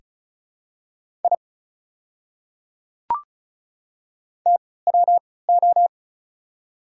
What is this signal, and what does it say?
Morse code, heard as a keyed tone, about 34 wpm, spaced as for 20 wpm, 690 Hz. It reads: I TWO
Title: 3x Character Spacing - Top 500 Words - 35wpm